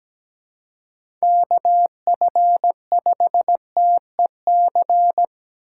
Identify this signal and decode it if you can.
Morse code, heard as a keyed tone, about 17 wpm, 700 Hz. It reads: KF5TEC
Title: USA Call Signs - 6 Characters - 6x Character Spacing - 17wpm